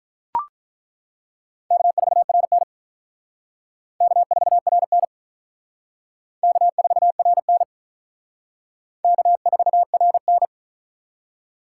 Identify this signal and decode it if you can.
Morse code, heard as a keyed tone, about 42 wpm, 700 Hz. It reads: K4RN K4RN K4RN K4RN
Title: Speed Racing - USA 1x2 Call Signs 35wpm